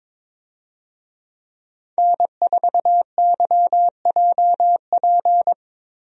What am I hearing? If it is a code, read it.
Morse code, heard as a keyed tone, about 21 wpm, 700 Hz. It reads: N4YJP
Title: USA Call Signs - 5 Characters - Repeated 3x - 22wpm